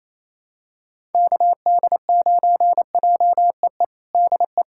Morse code, heard as a keyed tone, about 28 wpm, 700 Hz. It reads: KD9JEE DE
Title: USA Callsign DE Callsign - 6 Characters - 28wpm